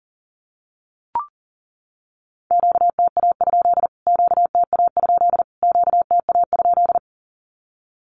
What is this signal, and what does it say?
Morse code, heard as a keyed tone, about 40 wpm, 700 Hz. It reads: QTA? QTA? QTA?
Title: Q-Codes Repeated 3x 40wpm